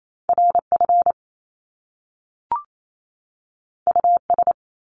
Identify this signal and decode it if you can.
Morse code, heard as a keyed tone, about 28 wpm, 700 Hz. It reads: RF US
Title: Top Two Letter Combinations in Words 28wpm